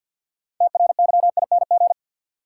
Morse code, heard as a keyed tone, about 52 wpm, 700 Hz. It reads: TRYING
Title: Sets of 2 Words - New Word Drill - 50wpm